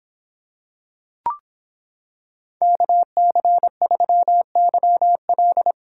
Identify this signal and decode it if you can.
Morse code, heard as a keyed tone, about 26 wpm, 700 Hz. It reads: KC3YL